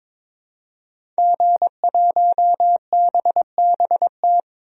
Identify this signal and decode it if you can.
Morse code, heard as a keyed tone, about 22 wpm, 700 Hz. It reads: G1BBT
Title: DXCC-100 Call Signs - 5 Characters - 22wpm